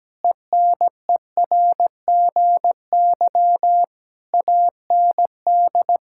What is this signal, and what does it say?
Morse code, heard as a keyed tone, about 17 wpm, 700 Hz. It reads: ENERGY AND
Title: Sentences from Top 1000 Words - Review Part 6 - 17wpm